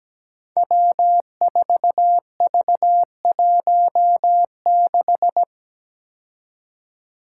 Morse code, heard as a keyed tone, about 17 wpm, 700 Hz. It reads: W4V16
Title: Five Challenging Letter-Number Combinations 17wpm